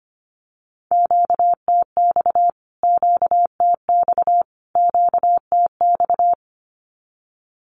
Morse code, heard as a keyed tone, about 25 wpm, 700 Hz. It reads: QTX QTX QTX
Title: Q-Codes Repeated 3x 25wpm